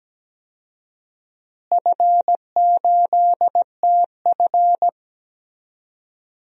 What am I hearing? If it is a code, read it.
Morse code, heard as a keyed tone, about 17 wpm, 700 Hz. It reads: F8TF
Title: DXCC-100 Call Signs - 4 Characters - Repeated 2x - 17wpm